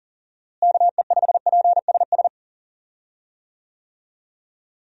Morse code, heard as a keyed tone, about 39 wpm, 700 Hz. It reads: KE5PSS
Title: USA Call Signs - 6 Characters - Repeated 3x - 40wpm